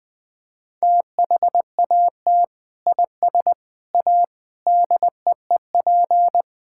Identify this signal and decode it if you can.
Morse code, heard as a keyed tone, about 20 wpm, 700 Hz. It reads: THAT IS A DEEP
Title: Sentences from Top 1600 Words 20wpm